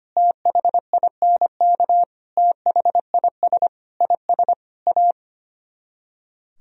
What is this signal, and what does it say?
Morse code, heard as a keyed tone, about 29 wpm, 700 Hz. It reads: THINK THIS IS A
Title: Speed-Racing - Sets of 5 Words 25wpm